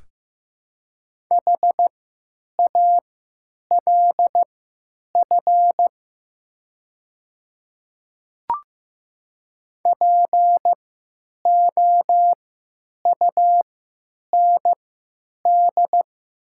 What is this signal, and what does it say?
Morse code, heard as a keyed tone, about 15 wpm, spaced as for 9 wpm, 700 Hz. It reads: HALF POUND
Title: 3x Character Spacing - Top 500 Words - 15wpm